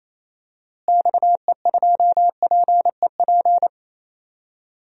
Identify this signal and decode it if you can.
Morse code, heard as a keyed tone, about 28 wpm, 700 Hz. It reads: XE2PEP
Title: DXCC-100 Call Signs - 6 Characters - Repeated 2x - 28wpm